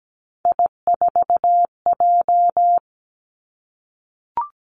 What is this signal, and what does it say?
Morse code, heard as a keyed tone, about 17 wpm, 700 Hz. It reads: I4J
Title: Three Challenging Letter-Number Combinations 17wpm